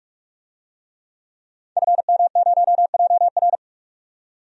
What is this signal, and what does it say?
Morse code, heard as a keyed tone, about 46 wpm, 700 Hz. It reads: FM0JR
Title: DXCC-100 Call Signs - 5 Characters - 45wpm